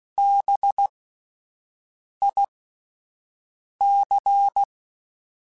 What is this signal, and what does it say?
Morse code, heard as a keyed tone, about 16 wpm, spaced as for 5 wpm, 800 Hz. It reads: BIC
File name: Alphabetical / Alphabetical mp3s / Bs @5